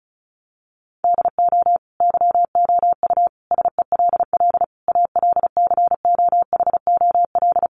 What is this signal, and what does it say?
Morse code, heard as a keyed tone, about 34 wpm, 700 Hz. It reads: DO YOU SELL ALCOHOL
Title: Sentences from Top 1400 Words 35wpm